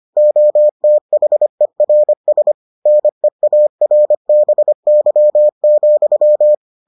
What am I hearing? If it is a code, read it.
Morse code, heard as a keyed tone, about 25 wpm, 600 Hz. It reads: OTHERS NEARBY,